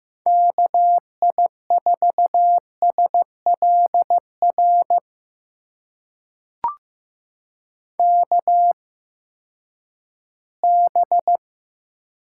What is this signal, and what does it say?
Morse code, heard as a keyed tone, about 15 wpm, spaced as for 10 wpm, 700 Hz. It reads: KI4SLR KB
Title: USA Call Signs - 6 Characters - 8x Character Spacing - 15wpm